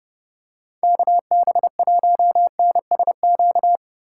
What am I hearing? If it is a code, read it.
Morse code, heard as a keyed tone, about 30 wpm, 700 Hz. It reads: KB1NSQ
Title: USA Call Signs - 4x Character Spacing - 30wpm